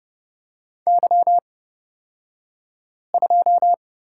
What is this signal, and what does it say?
Morse code, heard as a keyed tone, about 30 wpm, spaced as for 5 wpm, 690 Hz. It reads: Y2
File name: Lesson 36d - BT - Two Characters 30wpm